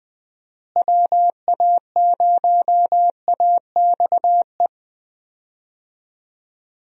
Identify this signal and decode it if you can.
Morse code, heard as a keyed tone, about 20 wpm, 700 Hz. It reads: WA0AXE
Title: USA Call Signs - 6 Characters - Repeated 2x - 20wpm